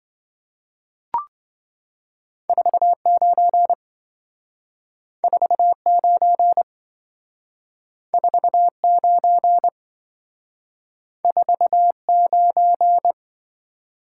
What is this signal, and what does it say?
Morse code, heard as a keyed tone, about 26 wpm, 700 Hz. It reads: 49 49 49 49